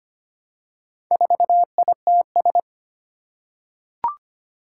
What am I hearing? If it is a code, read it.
Morse code, heard as a keyed tone, about 25 wpm, 700 Hz. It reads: 4ITS
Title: Four Challenging Letter-Number Combinations 25wpm